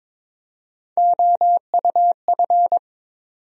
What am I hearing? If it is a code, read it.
Morse code, heard as a keyed tone, about 22 wpm, 700 Hz. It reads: OUF
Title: USA Call Sign Postfixes - 3 Letters 22wpm